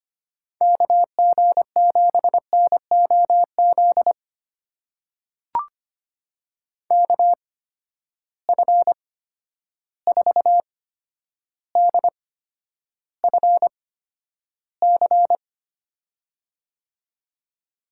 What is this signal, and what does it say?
Morse code, heard as a keyed tone, about 25 wpm, spaced as for 11 wpm, 700 Hz. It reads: KG7NOZ KF4DFC
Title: USA Call Signs - 6 Characters - 8x Character Spacing - 25wpm